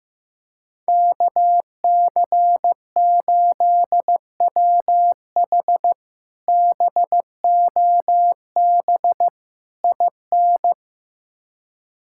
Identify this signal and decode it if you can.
Morse code, heard as a keyed tone, about 15 wpm, 700 Hz. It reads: KC8WH BOB IN
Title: K1USN SST QSO Practice - 15wpm